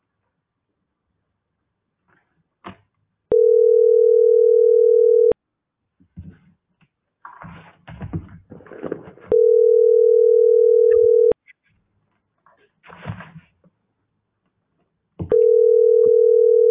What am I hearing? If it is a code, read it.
Ringback tone (North America), 440 + 480 Hz.